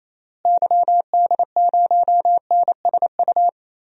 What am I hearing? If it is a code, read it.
Morse code, heard as a keyed tone, about 28 wpm, 700 Hz. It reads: YD0NSU
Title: DXCC-100 Call Signs - 6 Characters - Repeated 3x - 28wpm